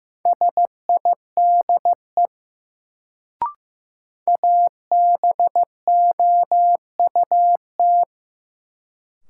Morse code, heard as a keyed tone, about 15 wpm, 700 Hz. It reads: SIDE ABOUT
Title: Sets of 2 Words - New Word Drill - 15wpm